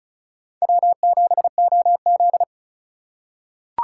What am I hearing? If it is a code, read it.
Morse code, heard as a keyed tone, about 34 wpm, 700 Hz. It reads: W7OZ
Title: USA Call Signs - 4 Characters - 6x Character Spacing - 35wpm